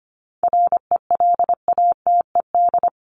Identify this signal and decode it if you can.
Morse code, heard as a keyed tone, about 25 wpm, 700 Hz. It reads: RELATED
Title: ED Suffix - Top 100 Words with Base Words 25wpm